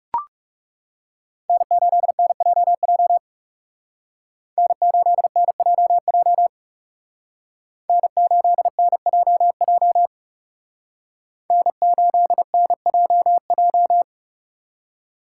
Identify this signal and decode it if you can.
Morse code, heard as a keyed tone, about 38 wpm, 700 Hz. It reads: N8NJJ N8NJJ N8NJJ N8NJJ